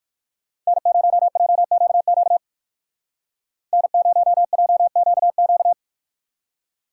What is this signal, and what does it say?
Morse code, heard as a keyed tone, about 54 wpm, 690 Hz. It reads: N0JQQ N0JQQ
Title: Speed Racing - USA 1x3 Call Signs 45wpm